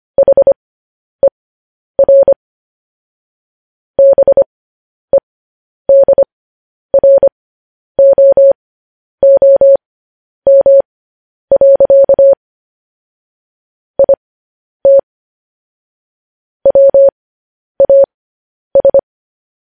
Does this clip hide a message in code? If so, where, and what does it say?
Yes: Morse code from 0.2 to 19.0 s, heard as a keyed tone, about 25 wpm, spaced as for 10 wpm, 550 Hz. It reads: HER BEDROOM. IT WAS